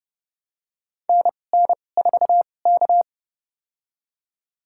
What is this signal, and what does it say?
Morse code, heard as a keyed tone, about 30 wpm, spaced as for 22 wpm, 700 Hz. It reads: NN4K
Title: USA Call Signs - 4 Characters - 2x Character Spacing - 30wpm